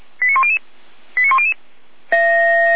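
Five-tone selective call (ZVEI), heard as quick runs of five tones, tones of about 70 ms. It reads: 89100 89100
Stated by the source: ZVEI five-tone call received on SDR.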